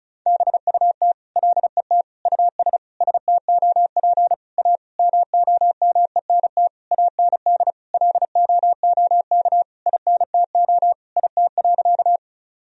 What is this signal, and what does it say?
Morse code, heard as a keyed tone, about 36 wpm, 700 Hz. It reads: BUT LET US STOP A MOMENT AND LOOK INTO IT.
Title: Sentences from Top 1000 Words - Review Part 1 - 35wpm